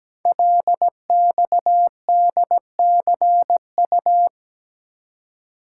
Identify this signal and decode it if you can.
Morse code, heard as a keyed tone, about 17 wpm, 700 Hz. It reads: LXDCU
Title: Code Groups - 5 Letters - 17wpm